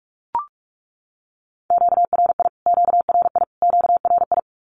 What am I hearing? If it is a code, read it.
Morse code, heard as a keyed tone, about 44 wpm, 700 Hz. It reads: QRI QRI QRI
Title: Q-Codes Repeated 3x 45wpm